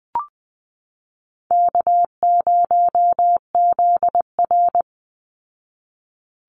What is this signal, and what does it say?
Morse code, heard as a keyed tone, about 19 wpm, 700 Hz. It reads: K0ZR
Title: Call Signs Repeated 3x - 20wpm